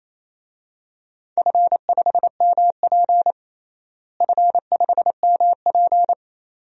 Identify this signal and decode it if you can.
Morse code, heard as a keyed tone, about 28 wpm, 700 Hz. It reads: F5MP F5MP
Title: DXCC-100 Call Signs - 4 Characters - Repeated 2x - 28wpm